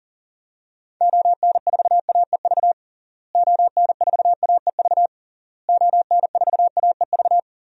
Morse code, heard as a keyed tone, about 37 wpm, 700 Hz. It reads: ON4AEV ON4AEV ON4AEV
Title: DXCC-100 Call Signs - 6 Characters - Repeated 3x - 40wpm